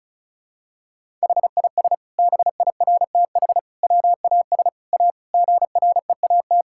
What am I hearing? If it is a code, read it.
Morse code, heard as a keyed tone, about 36 wpm, 700 Hz. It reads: HIS BIRTH WAS A GREAT
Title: Sentences from Top 1000 Words - Review Part 5 - 35wpm